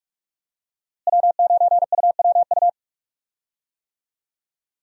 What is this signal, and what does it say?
Morse code, heard as a keyed tone, about 46 wpm, 700 Hz. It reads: W9UWU
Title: USA Call Signs - 5 Characters - Repeated 2x - 45wpm